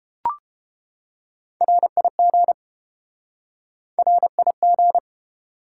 Morse code, heard as a keyed tone, about 32 wpm, 700 Hz. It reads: RIG RIG